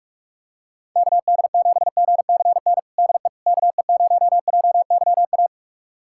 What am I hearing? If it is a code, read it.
Morse code, heard as a keyed tone, about 46 wpm, 700 Hz. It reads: KD7GCN DE KE0JYA
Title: USA Callsign DE Callsign 45wpm